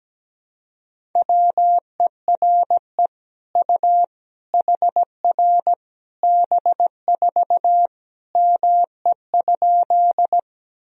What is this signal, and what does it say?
Morse code, heard as a keyed tone, about 17 wpm, 700 Hz. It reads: WERE U HR B4 ME?
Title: Sentences from Top 200 Words and Abbreviations 17wpm